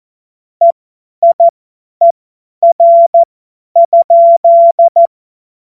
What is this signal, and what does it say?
Morse code, heard as a keyed tone, about 14 wpm, spaced as for 10 wpm, 680 Hz. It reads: EIER?
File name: Witz-15-Juli-2025-50-F70